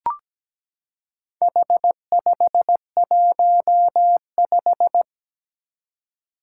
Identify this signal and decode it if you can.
Morse code, heard as a keyed tone, about 17 wpm, 700 Hz. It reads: H515
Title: Four Challenging Letter-Number Combinations 17wpm